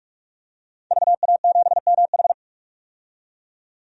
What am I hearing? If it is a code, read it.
Morse code, heard as a keyed tone, about 46 wpm, 700 Hz. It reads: VA7MH
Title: DXCC-100 Call Signs - 5 Characters - 45wpm